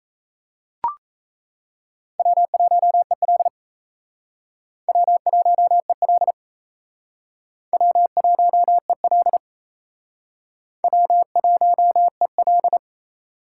Morse code, heard as a keyed tone, about 37 wpm, 700 Hz. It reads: W1EL W1EL W1EL W1EL